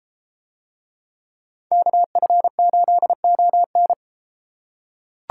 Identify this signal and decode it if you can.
Morse code, heard as a keyed tone, about 32 wpm, 700 Hz. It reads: KF8ON